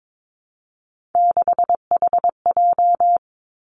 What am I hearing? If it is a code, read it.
Morse code, heard as a keyed tone, about 22 wpm, 700 Hz. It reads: 6HJ
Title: Three Challenging Letter-Number Combinations 22wpm